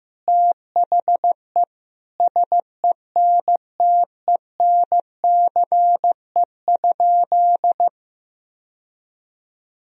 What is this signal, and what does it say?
Morse code, heard as a keyed tone, about 16 wpm, 700 Hz. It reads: THE SENTENCE?
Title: Sentences from Top 1000 Words - Review Part 1 - 15wpm